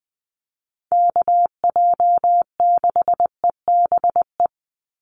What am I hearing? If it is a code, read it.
Morse code, heard as a keyed tone, about 20 wpm, 700 Hz. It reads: KJ6EBE
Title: USA Call Signs 20wpm